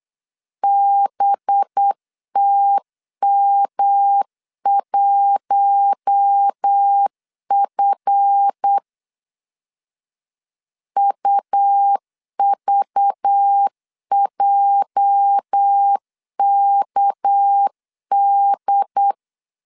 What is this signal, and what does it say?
Morse code, heard as a keyed tone, about 8 wpm, 800 Hz. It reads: BTM1F UVJKD
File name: LEZIONI / 8 YJ45 / 8 YJ4 8wpm 3246